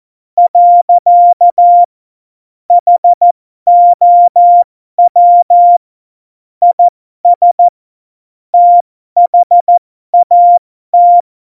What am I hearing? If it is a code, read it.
Morse code, heard as a keyed tone, about 14 wpm, spaced as for 12 wpm, 700 Hz. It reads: . HOW IS THAT
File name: Morsekurs / Witze